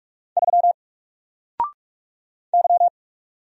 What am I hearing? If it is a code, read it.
Morse code, heard as a keyed tone, about 46 wpm, 700 Hz. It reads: 3 Y
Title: Single Letter-Number Alt 45wpm